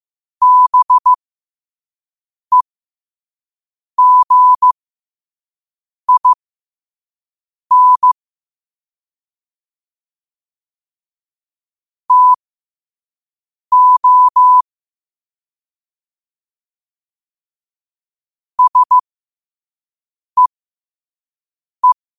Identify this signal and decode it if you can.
Morse code, heard as a keyed tone, about 15 wpm, spaced as for 5 wpm, 1000 Hz. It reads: BEGIN TO SEE
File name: qotdfiles / 5wpm